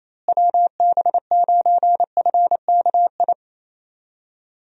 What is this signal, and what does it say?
Morse code, heard as a keyed tone, about 27 wpm, 700 Hz. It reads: WB9FKI
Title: USA Call Signs - 6 Characters - Repeated 3x - 28wpm